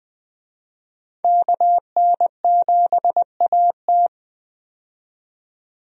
Morse code, heard as a keyed tone, about 20 wpm, 700 Hz. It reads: KN7AT
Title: USA Call Signs - 5 Characters - 20wpm